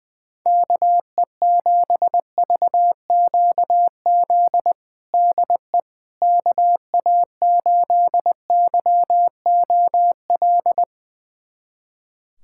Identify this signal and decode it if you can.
Morse code, heard as a keyed tone, about 20 wpm, 700 Hz. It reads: KE7VQZ DE KA8YOL
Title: USA Callsign DE Callsign - 6 Characters - 20wpm